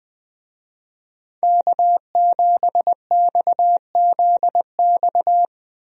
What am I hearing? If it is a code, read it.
Morse code, heard as a keyed tone, about 20 wpm, 700 Hz. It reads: K7XZX
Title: USA Call Signs - 5 Characters - 20wpm